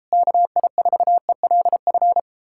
Morse code, heard as a keyed tone, about 37 wpm, 700 Hz. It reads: KI4ELF